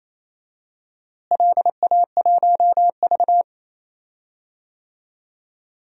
Morse code, heard as a keyed tone, about 28 wpm, 700 Hz. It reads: LA1V
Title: DXCC-100 Call Signs - 4 Characters - Repeated 3x - 28wpm